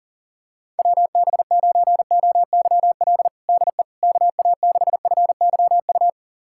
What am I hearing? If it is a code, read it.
Morse code, heard as a keyed tone, about 40 wpm, 700 Hz. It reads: WB9OYL DE KA6FYU
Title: USA Callsign DE Callsign - 6 Characters - 40wpm